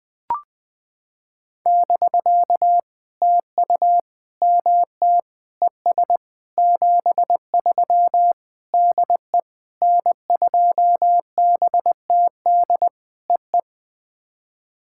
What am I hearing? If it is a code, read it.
Morse code, heard as a keyed tone, about 20 wpm, 700 Hz. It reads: <BK> TU MT ES 73 DE N2BTD EE
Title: POTA QSO Practice - 20wpm